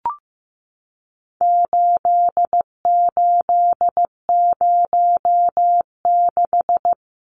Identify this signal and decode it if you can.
Morse code, heard as a keyed tone, about 15 wpm, 700 Hz. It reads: 8806
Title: Numbers - 4-Digit 15wpm